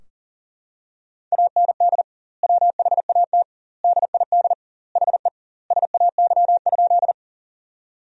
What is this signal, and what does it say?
Morse code, heard as a keyed tone, about 40 wpm, 700 Hz. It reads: AND WHAT DID HE SAY?
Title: 2x Word-Spacing - Sentences from Top 200 Words - 40wpm